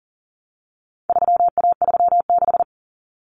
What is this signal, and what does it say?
Morse code, heard as a keyed tone, about 40 wpm, 700 Hz. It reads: 3A36
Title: Common Cut Numbers - 4-Digit 40wpm